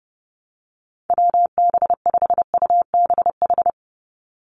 Morse code, heard as a keyed tone, about 30 wpm, 700 Hz. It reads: WB5UBH
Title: USA Call Signs 30wpm